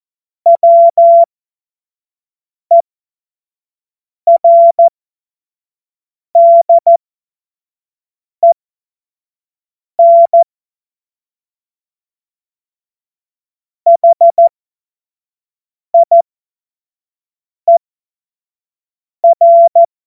Morse code, heard as a keyed tone, about 14 wpm, spaced as for 5 wpm, 680 Hz. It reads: WERDEN HIER
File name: Morsekurs / Witze / Witz 29. Juni 2021, 25 F70